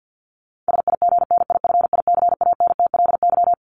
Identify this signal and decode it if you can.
Morse code, heard as a keyed tone, about 53 wpm, 700 Hz. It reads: SIGNIFICANTLY
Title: Top 1251-1500 Words 50wpm